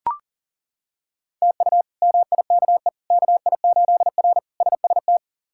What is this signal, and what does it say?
Morse code, heard as a keyed tone, about 40 wpm, 700 Hz. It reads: TU MIKE KI8R SST
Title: K1USN SST QSO Practice - 40wpm